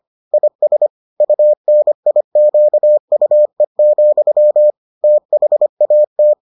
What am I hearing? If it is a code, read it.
Morse code, heard as a keyed tone, about 25 wpm, 600 Hz. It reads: IS UNIQUE, THAT